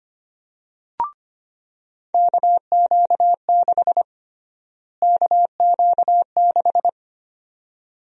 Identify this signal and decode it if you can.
Morse code, heard as a keyed tone, about 25 wpm, 700 Hz. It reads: KQ6 KQ6
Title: USA Call Sign Prefixes - Repeated 2x - 25wpm